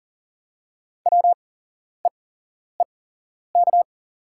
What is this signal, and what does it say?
Morse code, heard as a keyed tone, about 40 wpm, spaced as for 11 wpm, 690 Hz. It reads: WEEK
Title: 8x Character Spacing - Top 500 Words - 40wpm